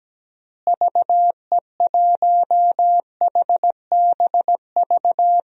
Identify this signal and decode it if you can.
Morse code, heard as a keyed tone, about 17 wpm, 700 Hz. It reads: VE1HBV
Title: DXCC-100 Call Signs - 6 Characters - Repeated 3x - 17wpm